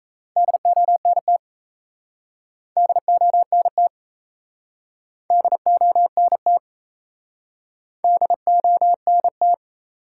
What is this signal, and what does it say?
Morse code, heard as a keyed tone, about 37 wpm, 700 Hz. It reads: DONT DONT DONT DONT